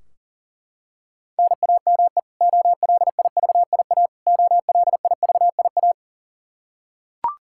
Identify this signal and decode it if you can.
Morse code, heard as a keyed tone, about 40 wpm, 700 Hz. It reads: NAME OLIVIA OLIVIA
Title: Practice QSO - Exchange 1 - Beginning - 40wpm